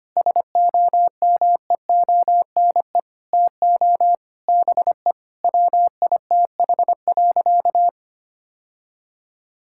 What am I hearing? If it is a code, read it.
Morse code, heard as a keyed tone, about 25 wpm, 700 Hz. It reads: SOMEONE TO BE WITH.
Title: Sentences from Top 1000 Words - Review Part 5 - 25wpm